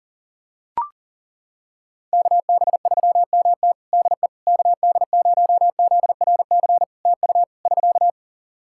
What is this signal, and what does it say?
Morse code, heard as a keyed tone, about 37 wpm, 700 Hz. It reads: KB3MT DE KD0ZRC TU <SK>
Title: Practice QSO - Exchange 4 - Ending - 40wpm